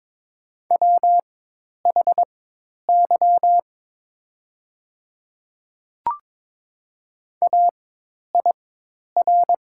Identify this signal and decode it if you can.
Morse code, heard as a keyed tone, about 22 wpm, spaced as for 10 wpm, 700 Hz. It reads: WHY AIR
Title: 4x Character Spacing - Top 200 Words - 22wpm